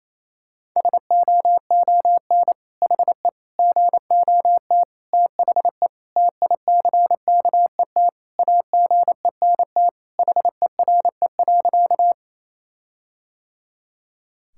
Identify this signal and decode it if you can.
Morse code, heard as a keyed tone, about 28 wpm, 700 Hz. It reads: SOON HE GOT THE TICKET AGENT HERE.
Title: Sentences from Top 1600 Words 28wpm